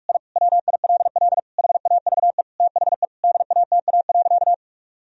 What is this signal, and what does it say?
Morse code, heard as a keyed tone, about 46 wpm, 700 Hz. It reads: I WILL HAVE THE DATA.
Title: Sentences from Top 1000 Words - Review Part 3 - 45wpm